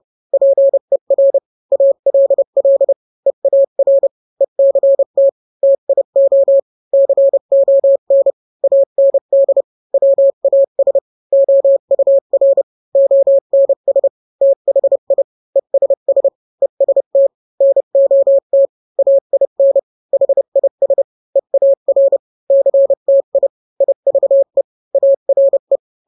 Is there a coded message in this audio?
Yes: Morse code from 0.3 to 25.8 s, heard as a keyed tone, about 30 wpm, 550 Hz. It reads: PER ALL EAR ECT TIO CON AND WAS OUR ONS THI ESS EST NOT AIN HIS EAR CTI IVE ARE